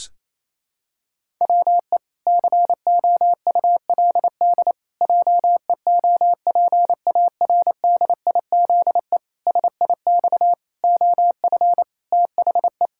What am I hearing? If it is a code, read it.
Morse code, heard as a keyed tone, about 28 wpm, 700 Hz. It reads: WE COULD JEOPARDIZE SIX OF THE
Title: Pangram Challenge v2 - 28wpm